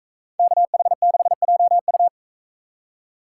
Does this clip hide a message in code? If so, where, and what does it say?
Yes: Morse code from 0.4 to 2.1 s, heard as a keyed tone, about 40 wpm, 700 Hz. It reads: KH6JU